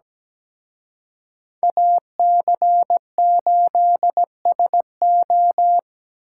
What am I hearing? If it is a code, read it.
Morse code, heard as a keyed tone, about 20 wpm, 700 Hz. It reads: AC8SO
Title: Speed Racing - USA 2x2 Call Signs 17wpm